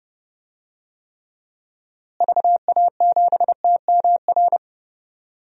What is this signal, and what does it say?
Morse code, heard as a keyed tone, about 29 wpm, 700 Hz. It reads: VA7TMR
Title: DXCC-100 Call Signs - 6 Characters - Repeated 3x - 30wpm